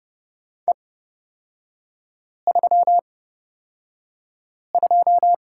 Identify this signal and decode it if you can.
Morse code, heard as a keyed tone, about 30 wpm, spaced as for 5 wpm, 700 Hz. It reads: E32
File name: Lesson 36e - BT - Three Characters 30wpm